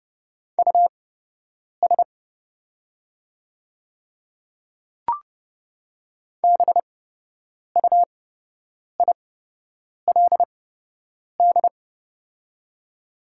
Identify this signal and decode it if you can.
Morse code, heard as a keyed tone, about 30 wpm, spaced as for 8 wpm, 700 Hz. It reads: US BUILD
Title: 8x Character Spacing - Top 200 Words - 30wpm